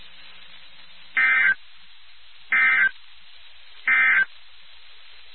Emergency Alert System (SAME) tones, heard as harsh buzzing bursts. It is heard from 1.0 to 4.3 s.
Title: NOAA Weather Radio WXK79